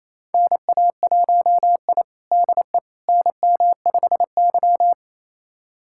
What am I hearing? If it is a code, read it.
Morse code, heard as a keyed tone, about 28 wpm, 700 Hz. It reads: NA1I DE NM5Y
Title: USA Callsign DE Callsign - 4 Characters - 28wpm